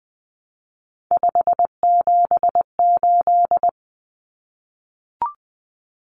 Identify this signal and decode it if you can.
Morse code, heard as a keyed tone, about 20 wpm, 700 Hz. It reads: 578
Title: Numbers - 3-Digit 20wpm